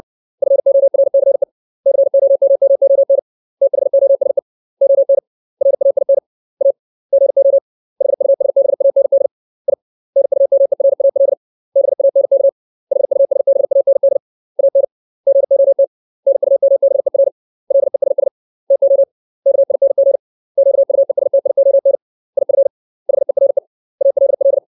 Morse code, heard as a keyed tone, about 60 wpm, 550 Hz. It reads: FORCE COMMON THOSE ON RAIN A GO HUSBAND I NUMERAL BANK HUSBAND AN GOT NUMBER LESS TO KING QUESTION IF HERE ALL
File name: 632 words / 632 Words 60WPM